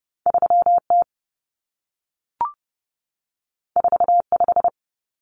Morse code, heard as a keyed tone, about 30 wpm, 700 Hz. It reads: 3T 45
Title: Common Cut Numbers - 2-Digit 30wpm